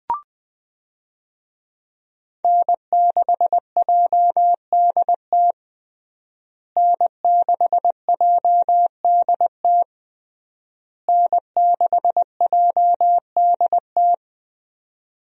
Morse code, heard as a keyed tone, about 20 wpm, 700 Hz. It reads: N6JDT N6JDT N6JDT
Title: USA Call Signs - 5 Characters - Repeated 3x - 20wpm